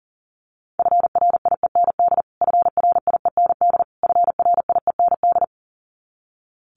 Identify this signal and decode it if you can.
Morse code, heard as a keyed tone, about 37 wpm, 700 Hz. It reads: FRIEND FRIEND FRIEND
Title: Top 300 Words Repeated 40wpm